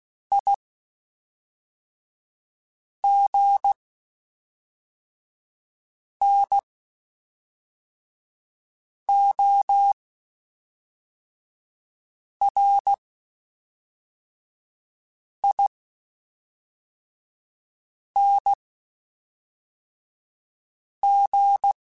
Morse code, heard as a keyed tone, about 16 wpm, spaced as for 3 wpm, 800 Hz. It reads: IGNORING
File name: Quotes Spaced / Quotes Spaced-mp3s / IgnoringFacts sp @5